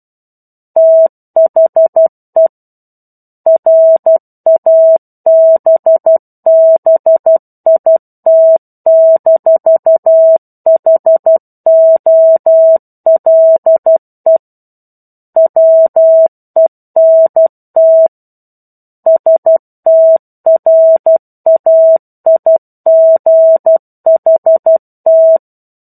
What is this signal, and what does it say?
Morse code, heard as a keyed tone, about 12 wpm, 650 Hz. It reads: THE RABBIT-HOLE WENT STRAIGHT
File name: 01-Chapter-01 12 WPM